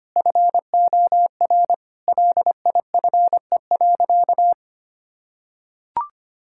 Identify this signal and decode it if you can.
Morse code, heard as a keyed tone, about 25 wpm, 700 Hz. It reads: FOR LIFE.
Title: Sentences from Top 1000 Words - Review Part 6 - 25wpm